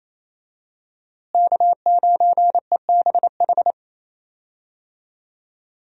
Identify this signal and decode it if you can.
Morse code, heard as a keyed tone, about 28 wpm, 700 Hz. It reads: K9EBH
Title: USA Call Signs - 5 Characters - Repeated 2x - 28wpm